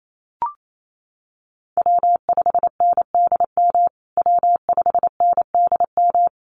Morse code, heard as a keyed tone, about 28 wpm, 700 Hz. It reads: W5NDM W5NDM
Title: USA Call Signs Repeated 2x 28wpm